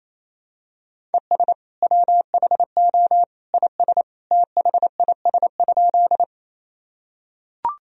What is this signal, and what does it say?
Morse code, heard as a keyed tone, about 27 wpm, 700 Hz. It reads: ES WHO IS THIS?
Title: Sentences from Top 100 Words and Abbreviations 28wpm